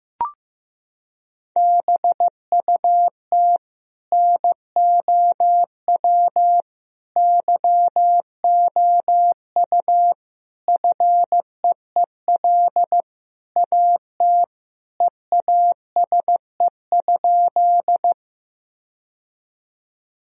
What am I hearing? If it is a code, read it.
Morse code, heard as a keyed tone, about 16 wpm, 700 Hz. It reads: BUT NOW YOU FEEL AT EASE?
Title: Sentences from Top 1000 Words - Review Part 1 - 15wpm